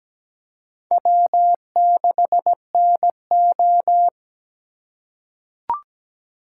Morse code, heard as a keyed tone, about 17 wpm, 700 Hz. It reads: W6NO
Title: USA Call Signs - 4 Characters - 4x Character Spacing - 17wpm